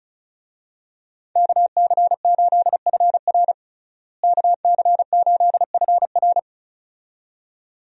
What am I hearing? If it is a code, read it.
Morse code, heard as a keyed tone, about 36 wpm, 700 Hz. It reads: KC8FR KC8FR
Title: USA Call Signs - 5 Characters - Repeated 2x - 35wpm